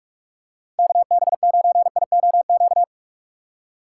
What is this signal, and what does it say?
Morse code, heard as a keyed tone, about 44 wpm, 700 Hz. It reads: KB9IOQ